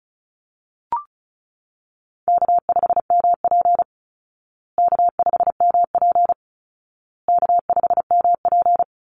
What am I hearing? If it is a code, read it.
Morse code, heard as a keyed tone, about 34 wpm, 700 Hz. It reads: K5MP K5MP K5MP
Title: Call Signs Repeated 2x - 35wpm